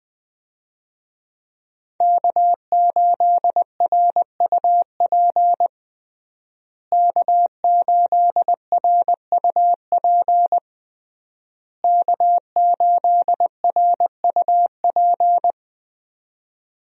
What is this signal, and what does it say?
Morse code, heard as a keyed tone, about 21 wpm, 700 Hz. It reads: K8RUP K8RUP K8RUP
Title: USA Call Signs - 5 Characters - Repeated 3x - 20wpm